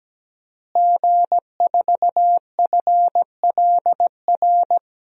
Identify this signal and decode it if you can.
Morse code, heard as a keyed tone, about 17 wpm, 700 Hz. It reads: G4FLR
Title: DXCC-100 Call Signs - 5 Characters - Repeated 3x - 17wpm